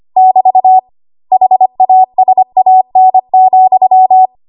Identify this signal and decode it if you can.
Morse code, heard as a keyed tone, about 25 wpm, 750 Hz. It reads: <BT> HASAN,
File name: Morse / 25WPM